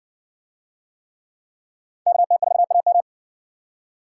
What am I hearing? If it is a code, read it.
Morse code, heard as a keyed tone, about 60 wpm, 700 Hz. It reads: KT4AM